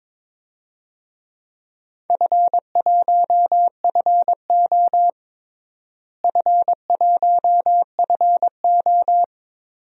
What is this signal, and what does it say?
Morse code, heard as a keyed tone, about 23 wpm, 700 Hz. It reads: F1FO F1FO
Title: DXCC-100 Call Signs - 4 Characters - Repeated 3x - 22wpm